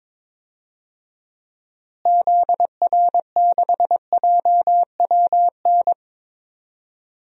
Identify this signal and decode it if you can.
Morse code, heard as a keyed tone, about 22 wpm, 700 Hz. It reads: ZR6JWN
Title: DXCC-100 Call Signs - 6 Characters - Repeated 2x - 22wpm